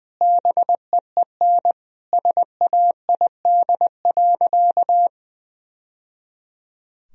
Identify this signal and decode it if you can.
Morse code, heard as a keyed tone, about 20 wpm, 700 Hz. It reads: BEEN SAID.
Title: Sentences from Top 1000 Words - Review Part 1 - 20wpm